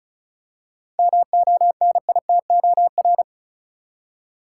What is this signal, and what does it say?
Morse code, heard as a keyed tone, about 34 wpm, 700 Hz. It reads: MONITOR
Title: Seven Letter Words - 35wpm